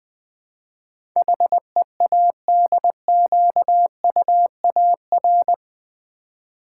Morse code, heard as a keyed tone, about 20 wpm, 700 Hz. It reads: HEADQUAR
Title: Build A Word - 4 Letters At A Time - 12-Letter Words - 20wpm